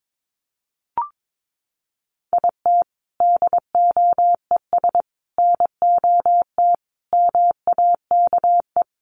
Morse code, heard as a keyed tone, about 22 wpm, 700 Hz. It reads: IT DOES NOT MAKE
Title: Sets of 5 Words Encore2 22wpm